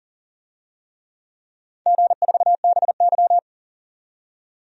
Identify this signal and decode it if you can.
Morse code, heard as a keyed tone, about 41 wpm, 700 Hz. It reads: G4BY
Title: DXCC-100 Call Signs - 4 Characters - 40wpm